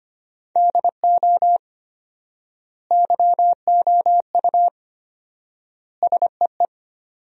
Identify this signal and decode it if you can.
Morse code, heard as a keyed tone, about 25 wpm, 700 Hz. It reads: DO YOU SEE
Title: 4x Word-Spacing - Sentences from Top 500 Words - 25wpm